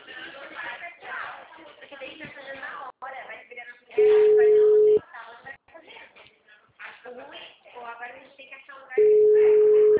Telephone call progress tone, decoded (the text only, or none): ringback tone